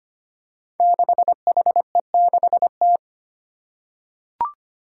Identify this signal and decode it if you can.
Morse code, heard as a keyed tone, about 25 wpm, 700 Hz. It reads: 6HE6T
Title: Five Challenging Letter-Number Combinations 25wpm